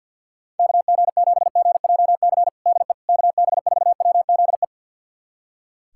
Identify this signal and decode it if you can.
Morse code, heard as a keyed tone, about 46 wpm, 690 Hz. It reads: KG7GJC DE KB4WZE
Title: USA Callsign DE Callsign 50wpm